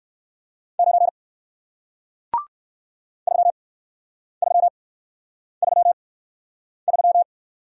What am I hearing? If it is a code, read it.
Morse code, heard as a keyed tone, about 59 wpm, spaced as for 9 wpm, 700 Hz. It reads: 9 3333